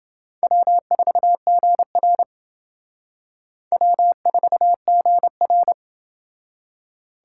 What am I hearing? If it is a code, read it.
Morse code, heard as a keyed tone, about 28 wpm, 700 Hz. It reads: W4GR W4GR